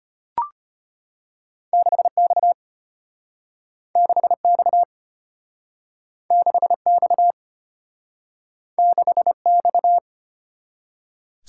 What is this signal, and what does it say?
Morse code, heard as a keyed tone, about 34 wpm, 700 Hz. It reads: 6X 6X 6X 6X